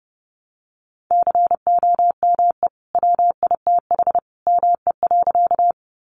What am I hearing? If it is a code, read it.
Morse code, heard as a keyed tone, about 30 wpm, 700 Hz. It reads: COME WITH ME.
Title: Sentences from Top 200 Words 30wpm